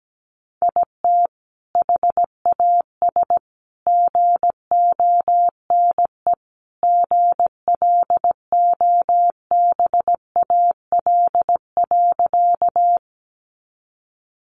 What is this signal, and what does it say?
Morse code, heard as a keyed tone, about 17 wpm, 700 Hz. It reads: IT HAS GONE GLOBAL.
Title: Sentences from Top 1200 Words 17wpm